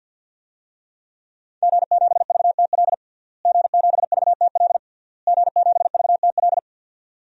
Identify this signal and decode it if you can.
Morse code, heard as a keyed tone, about 51 wpm, 700 Hz. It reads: G7VTL G7VTL G7VTL
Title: DXCC-100 Call Signs - 5 Characters - Repeated 3x - 50wpm